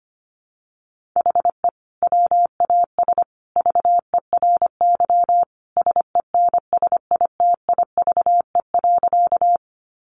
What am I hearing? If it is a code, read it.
Morse code, heard as a keyed tone, about 25 wpm, 700 Hz. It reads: HE WAS VERY SENSITIVE.
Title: Sentences from Top 1400 Words 25wpm